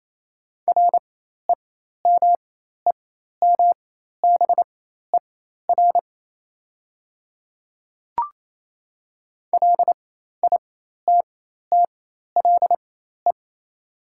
Morse code, heard as a keyed tone, about 28 wpm, spaced as for 13 wpm, 700 Hz. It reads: REMEMBER LITTLE
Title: 4x Character Spacing - Top 500 Words - 28wpm